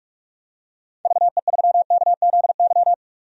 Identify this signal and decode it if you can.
Morse code, heard as a keyed tone, about 46 wpm, 700 Hz. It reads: VE3KZY
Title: DXCC-100 Call Signs - 6 Characters - Repeated 2x - 45wpm